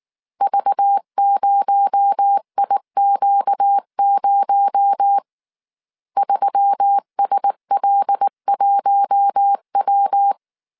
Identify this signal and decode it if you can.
Morse code, heard as a keyed tone, about 19 wpm, 800 Hz. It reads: V0IQ0 3SL1W
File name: CW / DET / 1 6 1500C S C 18wpm 2356